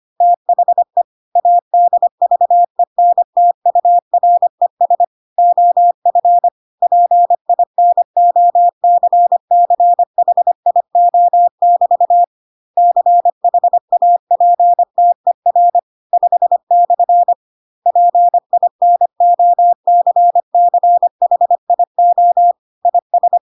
Morse code, heard as a keyed tone, about 25 wpm, 700 Hz. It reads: THE ADVENTURES OF PINOCCHIO<BT> CHAPTER 5/ PINOCCHIO IS
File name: Pinocchio / Pinocchio ch5 25 25 ik1hge